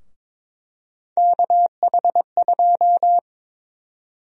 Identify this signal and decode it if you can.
Morse code, heard as a keyed tone, about 22 wpm, 700 Hz. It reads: KH2
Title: USA Call Sign Prefixes - Repeated 2x - 22wpm